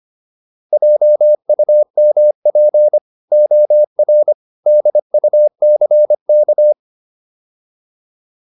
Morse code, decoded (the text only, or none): JUMP OR DUCK